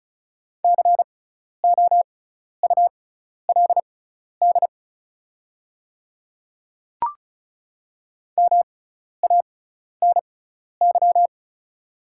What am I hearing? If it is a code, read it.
Morse code, heard as a keyed tone, about 36 wpm, spaced as for 12 wpm, 700 Hz. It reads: COULD MANY
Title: 6x Character Spacing - Top 100 Words - 35wpm